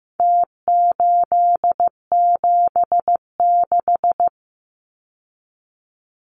Morse code, heard as a keyed tone, about 15 wpm, 700 Hz. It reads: T876
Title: Numbers - Contest Style - 4-Digit 15wpm